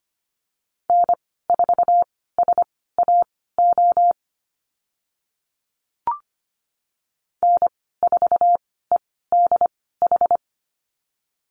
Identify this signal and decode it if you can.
Morse code, heard as a keyed tone, about 25 wpm, spaced as for 16 wpm, 700 Hz. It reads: N4SAO N4EDH
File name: Lesson 15g - Number 5 - Call Signs 25wpm